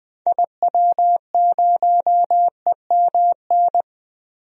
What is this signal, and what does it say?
Morse code, heard as a keyed tone, about 20 wpm, 700 Hz. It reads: IW0EMN
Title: DXCC-100 Call Signs - 6 Characters - 20wpm